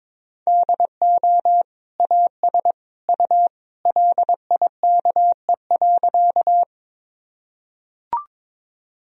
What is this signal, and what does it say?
Morse code, heard as a keyed tone, about 22 wpm, 700 Hz. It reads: DO AS U LIKE.
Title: Sentences from Top 100 Words and Abbreviations 22wpm